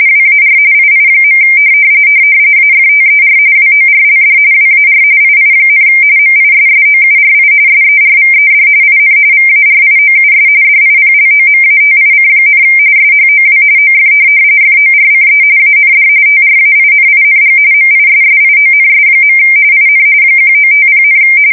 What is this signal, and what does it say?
RTTY, about 45.45 baud. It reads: WAR II. AFTER WORLD WAR II, MISSOURI SERVED IN VARIOUS DIPLOMATIC, SHOW OF FORCE AND TRAINING MISSIONS. IN 1950,
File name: fsk170 / USS-Missouri